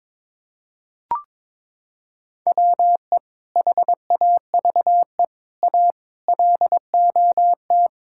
Morse code, heard as a keyed tone, about 22 wpm, 700 Hz. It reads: WE HAVE A LOT
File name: Sentences with 7-Word Phrases - 22wpm